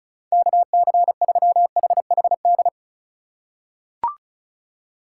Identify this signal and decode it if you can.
Morse code, heard as a keyed tone, about 36 wpm, 700 Hz. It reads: KC3HHD
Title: USA Call Signs - 6 Characters - 3x Character Spacing - 35wpm